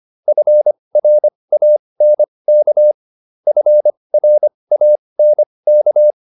Morse code, heard as a keyed tone, about 25 wpm, spaced as for 20 wpm, 600 Hz. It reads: FRANK FRANK